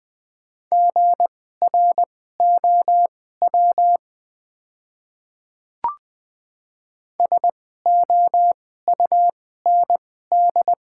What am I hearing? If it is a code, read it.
Morse code, heard as a keyed tone, about 20 wpm, spaced as for 14 wpm, 700 Hz. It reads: GROW SOUND
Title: 2x Character Spacing - Top 300 Words - 20wpm